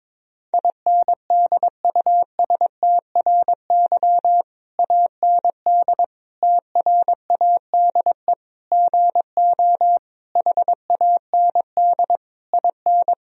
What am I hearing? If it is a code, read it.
Morse code, heard as a keyed tone, about 22 wpm, 700 Hz. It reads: INDUSTRY AND TRADE GO HAND IN
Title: Sentences from Top 1000 Words - Review Part 4 - 22wpm